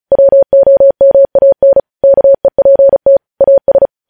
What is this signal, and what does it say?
Morse code, heard as a keyed tone, about 36 wpm, 550 Hz. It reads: WOMAN KEPT AS
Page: News Headlines in Morse Code at 35 WPM